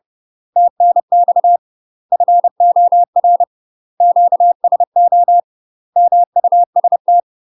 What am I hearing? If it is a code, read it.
Morse code, heard as a keyed tone, about 30 wpm, 700 Hz. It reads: TNX FOR QSO MUST